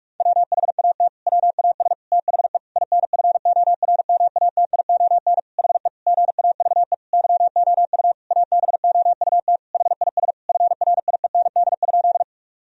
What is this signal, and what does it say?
Morse code, heard as a keyed tone, about 43 wpm, 700 Hz. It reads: WHAT WAS THE INFORMATION HE GAVE YOU ABOUT HIS FRIEND?
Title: Sentences from Top 1000 Words - Review Part 3 - 45wpm